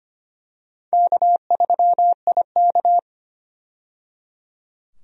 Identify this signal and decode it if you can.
Morse code, heard as a keyed tone, about 25 wpm, 700 Hz. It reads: K3IK
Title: USA Call Signs - 4 Characters - 25wpm